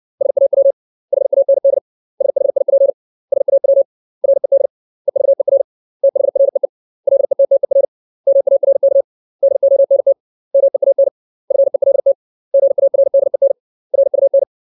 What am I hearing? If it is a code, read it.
Morse code, heard as a keyed tone, about 60 wpm, 550 Hz. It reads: SAW HAND SHIP SAW RED EVER THREE LETTER MARK DONT MEAN FELT GARDEN RUN
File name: 632 words / 632 Words 60WPM